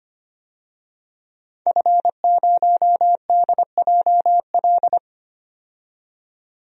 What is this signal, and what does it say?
Morse code, heard as a keyed tone, about 25 wpm, 700 Hz. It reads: F0DJL
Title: DXCC-100 Call Signs - 5 Characters - 25wpm